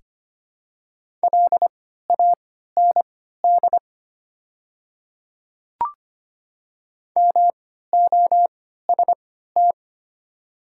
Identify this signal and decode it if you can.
Morse code, heard as a keyed tone, about 25 wpm, spaced as for 14 wpm, 700 Hz. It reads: LAND MOST
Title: 3x Character Spacing - Top 200 Words - 25wpm